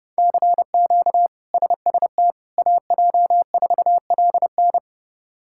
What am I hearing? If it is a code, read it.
Morse code, heard as a keyed tone, about 32 wpm, 700 Hz. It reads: CQ SST AJ4LN
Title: K1USN SST QSO Practice - 30wpm